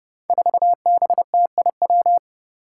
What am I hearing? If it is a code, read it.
Morse code, heard as a keyed tone, about 30 wpm, 700 Hz. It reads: 4BTIW
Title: Five Challenging Letter-Number Combinations 30wpm